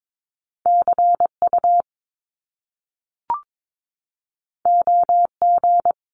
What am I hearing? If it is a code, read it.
Morse code, heard as a keyed tone, about 22 wpm, 700 Hz. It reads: CU OG
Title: Top Two Letter Combinations in Words 22wpm